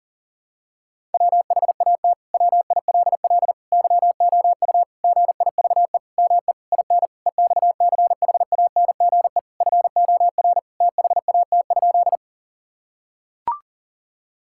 Morse code, heard as a keyed tone, about 40 wpm, 700 Hz. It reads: WHAT WILL YOU GIVE ME IN EXCHANGE FOR THAT?
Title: Sentences from Top 1000 Words - Review Part 6 - 40wpm